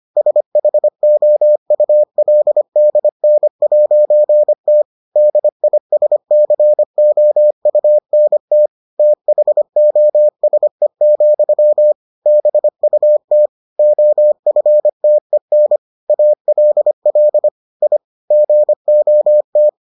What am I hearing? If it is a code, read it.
Morse code, heard as a keyed tone, about 25 wpm, 600 Hz. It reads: SHOULDN'T DISCOUNT THOSE, BUT OFTEN ALL I GOT